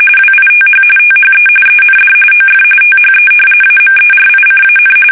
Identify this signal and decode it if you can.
RTTY, about 45.45 baud. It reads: WITH THE END OF THE AMERICAN
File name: fsk850 / usnavy1